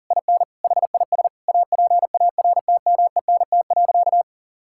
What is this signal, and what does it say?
Morse code, heard as a keyed tone, about 40 wpm, 700 Hz. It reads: IN HIS APARTMENT.
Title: Sentences from Top 1000 Words - Review Part 6 - 40wpm